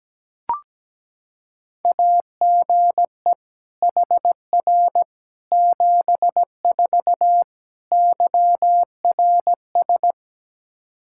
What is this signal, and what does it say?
Morse code, heard as a keyed tone, about 17 wpm, 700 Hz. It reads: AGE HR 74 YRS
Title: Practice QSO - Exchange 3 - More Info - 17wpm